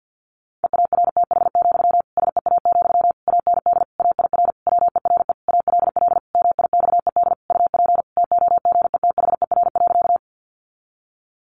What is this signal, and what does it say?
Morse code, heard as a keyed tone, about 50 wpm, 700 Hz. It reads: EARTH, SEA, AND AIR WERE ALL MIXED UP TOGETHER.
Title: Sentences from Top 1500 Words 50wpm